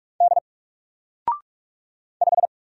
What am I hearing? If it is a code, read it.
Morse code, heard as a keyed tone, about 46 wpm, 700 Hz. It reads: D 5
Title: Single Letter-Number 45wpm